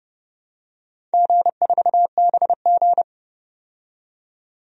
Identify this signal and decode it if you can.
Morse code, heard as a keyed tone, about 30 wpm, 700 Hz. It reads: G4BG
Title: DXCC-100 Call Signs - 4 Characters - 30wpm